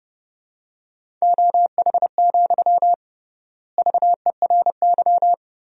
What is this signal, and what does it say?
Morse code, heard as a keyed tone, about 29 wpm, 700 Hz. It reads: OH, VERY
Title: 3x Word-Spacing - Sentences from Top 500 Words - 30wpm